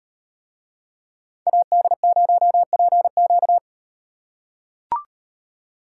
Morse code, heard as a keyed tone, about 38 wpm, 700 Hz. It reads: AD0PQ